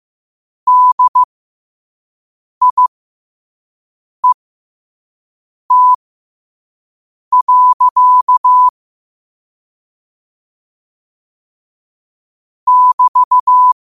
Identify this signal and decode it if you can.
Morse code, heard as a keyed tone, about 15 wpm, spaced as for 5 wpm, 1000 Hz. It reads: DIET. <BT>